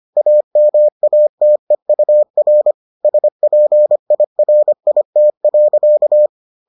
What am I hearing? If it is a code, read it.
Morse code, heard as a keyed tone, about 25 wpm, 600 Hz. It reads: AMATEUR SPIRIT.